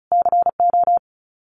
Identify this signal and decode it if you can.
Morse code, heard as a keyed tone, about 34 wpm, 700 Hz. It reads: CO
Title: US States Repeated 3x 35wpm